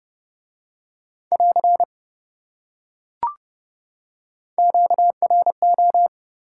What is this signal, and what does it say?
Morse code, heard as a keyed tone, about 28 wpm, 700 Hz. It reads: <AR> QRO